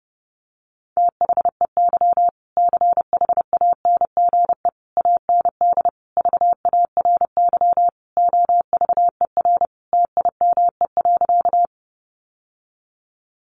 Morse code, heard as a keyed tone, about 30 wpm, 700 Hz. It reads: THEY CHANGE AND VARY OVER TIME.
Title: Sentences from Top 1200 Words 30wpm